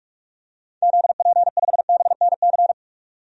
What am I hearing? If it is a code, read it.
Morse code, heard as a keyed tone, about 46 wpm, 700 Hz. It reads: ZP5BNC
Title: DXCC-100 Call Signs - 6 Characters - 45wpm